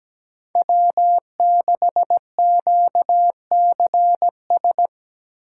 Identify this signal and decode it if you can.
Morse code, heard as a keyed tone, about 20 wpm, 700 Hz. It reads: W6QCS
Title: Speed Racing - USA 1x3 Call Signs 17wpm